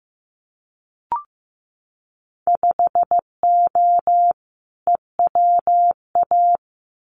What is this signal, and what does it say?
Morse code, heard as a keyed tone, about 15 wpm, 700 Hz. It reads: 5O EWA
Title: Winter Field Day Exchanges Encore 15wpm